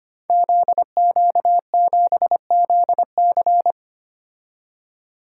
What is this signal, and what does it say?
Morse code, heard as a keyed tone, about 25 wpm, 700 Hz. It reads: ZQ7ZC
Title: Code Groups - 5 Letter:Numbers - 25wpm